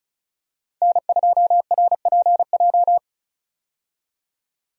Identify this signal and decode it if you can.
Morse code, heard as a keyed tone, about 36 wpm, 700 Hz. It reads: N2RPJ
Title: USA Call Signs - 5 Characters - 35wpm